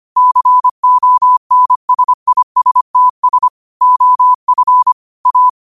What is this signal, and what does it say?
Morse code, heard as a keyed tone, about 25 wpm, 1000 Hz. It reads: CONSISTS OF A